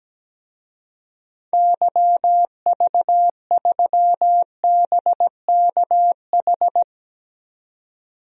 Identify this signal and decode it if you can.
Morse code, heard as a keyed tone, about 17 wpm, 700 Hz. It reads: YV3BKH
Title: DXCC-100 Call Signs - 6 Characters - 17wpm